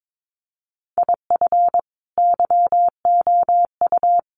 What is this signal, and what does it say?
Morse code, heard as a keyed tone, about 22 wpm, 700 Hz. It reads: IF YOU
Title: Sets of 5 Words 22wpm